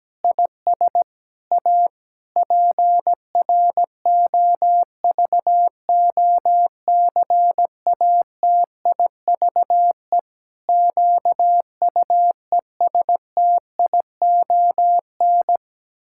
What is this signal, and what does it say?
Morse code, heard as a keyed tone, about 17 wpm, 700 Hz. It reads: IS A PROVOCATIVE QUESTION
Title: Pangram Challenge v2 - 17wpm